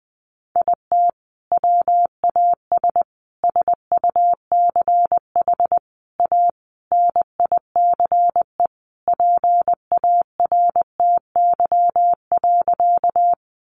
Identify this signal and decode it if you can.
Morse code, heard as a keyed tone, about 20 wpm, 700 Hz. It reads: IT WAS SUCH A NICE PARTY.
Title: Sentences from Top 1400 Words 20wpm